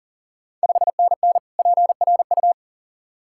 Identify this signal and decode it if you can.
Morse code, heard as a keyed tone, about 40 wpm, 690 Hz. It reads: 5NN PRU
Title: Ontario QSO Party Exchange 40wpm